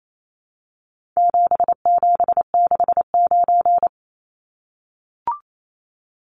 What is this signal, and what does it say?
Morse code, heard as a keyed tone, about 28 wpm, 700 Hz. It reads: 7769
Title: Numbers - 4-Digit 28wpm